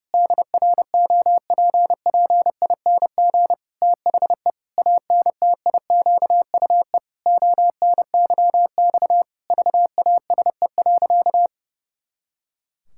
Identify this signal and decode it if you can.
Morse code, heard as a keyed tone, about 30 wpm, 700 Hz. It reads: DROPPING THE ANTIQUE ONYX VASE.
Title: Pangram Challenge v2 - 30wpm